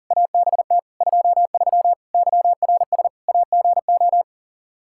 Morse code, heard as a keyed tone, about 40 wpm, 700 Hz. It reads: ABT 23 YRS AGO
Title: K1USN SST QSO Practice 40wpm